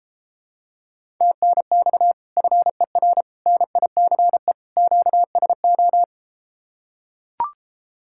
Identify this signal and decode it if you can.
Morse code, heard as a keyed tone, about 33 wpm, 700 Hz. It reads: TNX FER NICE QSO